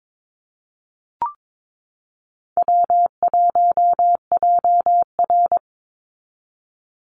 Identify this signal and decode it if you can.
Morse code, heard as a keyed tone, about 22 wpm, 700 Hz. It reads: W1JR
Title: Easy Call Signs Repeated 3x - 22wpm